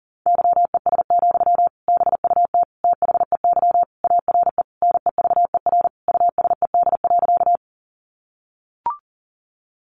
Morse code, heard as a keyed tone, about 40 wpm, 700 Hz. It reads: YES, BUT THEY ARE NEVER USED.
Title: Sentences from Top 600 Words 40wpm